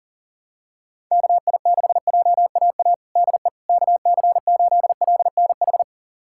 Morse code, heard as a keyed tone, about 37 wpm, 700 Hz. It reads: KI6JAA DE KC8LNH
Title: USA Callsign DE Callsign 40wpm